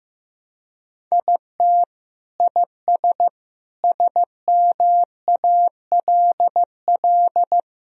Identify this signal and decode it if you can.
Morse code, heard as a keyed tone, about 15 wpm, 700 Hz. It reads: IT IS SMALL
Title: Sentences from Top 1000 Words - Review Part 1 - 15wpm